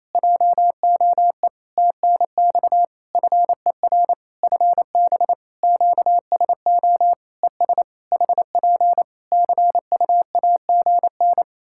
Morse code, heard as a keyed tone, about 28 wpm, 700 Hz. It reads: JOE TNX FER FB QSO ES HP CUAGN
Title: K1USN SST QSO Practice 28wpm